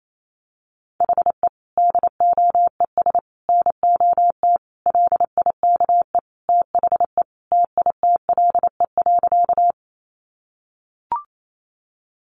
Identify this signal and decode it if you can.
Morse code, heard as a keyed tone, about 28 wpm, 700 Hz. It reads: HE DOES NOT LIKE THE TITLE.
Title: Sentences from Top 1200 Words 28wpm